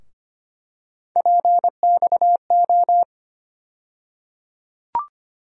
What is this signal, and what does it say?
Morse code, heard as a keyed tone, about 25 wpm, 700 Hz. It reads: PXO
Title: USA Call Sign Postfixes - 3 Letters 25wpm